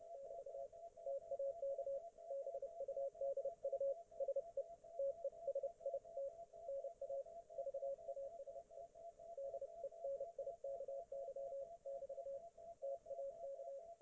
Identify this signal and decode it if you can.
Morse code, heard as a keyed tone, about 30 wpm, 550 Hz. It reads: <BT> TAK BUDU SE TESIT NA VASE DENIKY <BT> TAK